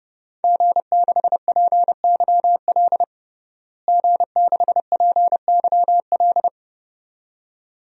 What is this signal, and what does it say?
Morse code, heard as a keyed tone, about 29 wpm, 700 Hz. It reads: G6PYL G6PYL
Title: DXCC-100 Call Signs - 5 Characters - Repeated 2x - 30wpm